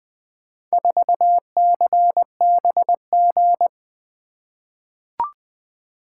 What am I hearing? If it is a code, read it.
Morse code, heard as a keyed tone, about 20 wpm, 700 Hz. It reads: 4CBG
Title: Four Letter-Number Combinations - 20wpm